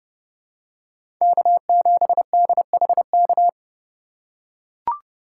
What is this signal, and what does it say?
Morse code, heard as a keyed tone, about 30 wpm, 700 Hz. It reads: K7DHK
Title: USA Call Signs - 5 Characters - 4x Character Spacing - 30wpm